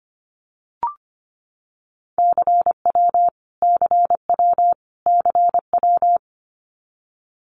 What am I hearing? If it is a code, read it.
Morse code, heard as a keyed tone, about 25 wpm, 700 Hz. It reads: CW CW CW
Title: Abbreviations Repeated 3x - 25wpm